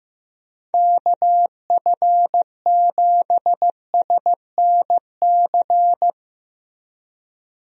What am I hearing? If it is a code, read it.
Morse code, heard as a keyed tone, about 16 wpm, 700 Hz. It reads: KF7SNC
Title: USA Call Signs - 6 Characters - Repeated 2x - 15wpm